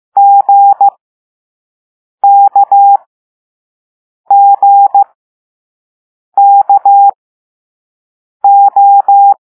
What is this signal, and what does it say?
Morse code, heard as a keyed tone, about 15 wpm, spaced as for 5 wpm, 800 Hz. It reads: GKGKO
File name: Morse / Level4A